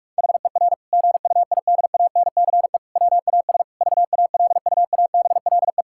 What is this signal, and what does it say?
Morse code, heard as a keyed tone, about 46 wpm, 700 Hz. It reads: HER GUIDANCE WAS VALUABLE
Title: Sentences from Top 1600 Words 45wpm